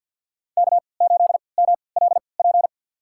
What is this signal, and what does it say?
Morse code, heard as a keyed tone, about 46 wpm, spaced as for 27 wpm, 700 Hz. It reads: K8MLP
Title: USA Call Signs - 5 Characters - 3x Character Spacing - 50wpm